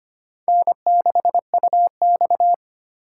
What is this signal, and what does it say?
Morse code, heard as a keyed tone, about 25 wpm, 700 Hz. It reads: N6UX
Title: USA Call Signs - 4 Characters - 8x Character Spacing - 25wpm